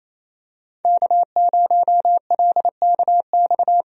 Morse code, heard as a keyed tone, about 28 wpm, 700 Hz. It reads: K0LKX
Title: USA Call Signs - 5 Characters - 28wpm